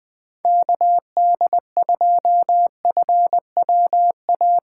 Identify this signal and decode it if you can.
Morse code, heard as a keyed tone, about 20 wpm, 700 Hz. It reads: KD2FWA
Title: USA Call Signs - 6 Characters - 2x Character Spacing - 20wpm